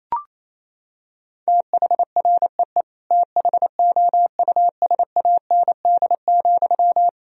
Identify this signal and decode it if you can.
Morse code, heard as a keyed tone, about 28 wpm, 700 Hz. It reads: THREE THOUSAND,
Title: Sentences from Top 1000 Words - Review Part 3 - 28wpm